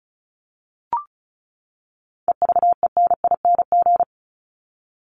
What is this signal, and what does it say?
Morse code, heard as a keyed tone, about 34 wpm, 700 Hz. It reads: EVENING
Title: Top 1501-1750 Words 35wpm